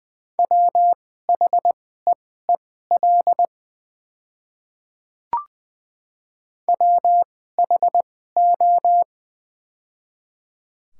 Morse code, heard as a keyed tone, about 20 wpm, spaced as for 14 wpm, 690 Hz. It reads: WHEEL WHO
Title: 2x Character Spacing - Top 500 Words - 20wpm